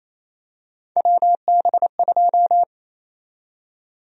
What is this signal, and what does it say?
Morse code, heard as a keyed tone, about 28 wpm, 700 Hz. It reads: WB2
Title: USA Call Sign Prefixes - Repeated 2x - 28wpm